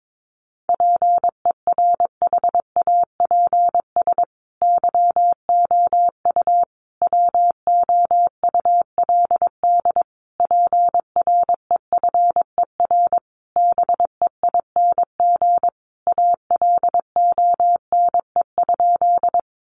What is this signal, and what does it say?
Morse code, heard as a keyed tone, about 22 wpm, 700 Hz. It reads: PERHAPS YOU WOULD PREFER BEING ALONE?
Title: Sentences from Top 1100 Words 22wpm